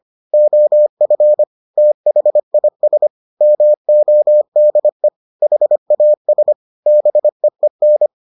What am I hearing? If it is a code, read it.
Morse code, heard as a keyed tone, about 25 wpm, 600 Hz. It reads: OF THIS MODE HAS BEEN